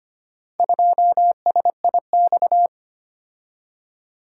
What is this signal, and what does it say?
Morse code, heard as a keyed tone, about 25 wpm, 700 Hz. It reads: 2SIX
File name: practice / 25wpm / Four Letter-Number Combinations - 25wpm